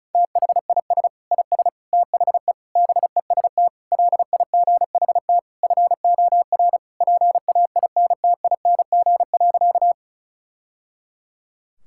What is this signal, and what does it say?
Morse code, heard as a keyed tone, about 36 wpm, 700 Hz. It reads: THIS IS THE BEST LIGHT FOR PAINTING.
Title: Sentences from Top 1000 Words - Review Part 6 - 35wpm